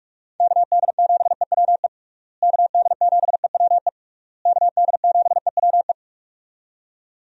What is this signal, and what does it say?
Morse code, heard as a keyed tone, about 46 wpm, 700 Hz. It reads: KD7EWE KD7EWE KD7EWE
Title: USA Call Signs - 6 Characters - Repeated 3x - 45wpm